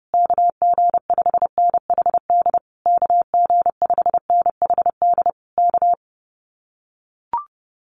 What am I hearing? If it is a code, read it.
Morse code, heard as a keyed tone, about 30 wpm, 700 Hz. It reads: KG5NHD KG5NHD K
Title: Common QSO Phrases Encore 30wpm